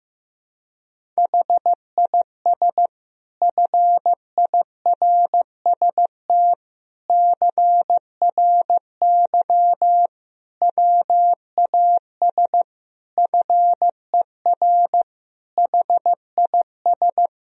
Morse code, heard as a keyed tone, about 15 wpm, 700 Hz. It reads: HIS FIRST CRY WAS FER HIS
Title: Sentences from Top 500 Words and Abbreviations 15wpm